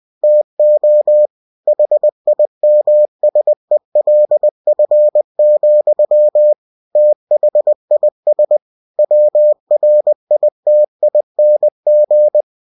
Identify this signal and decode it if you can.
Morse code, heard as a keyed tone, about 20 wpm, 600 Hz. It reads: TO HIMSELF, THIS WRITING
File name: cw-books / winnie-the-pooh / winnie-20wpm-page-0109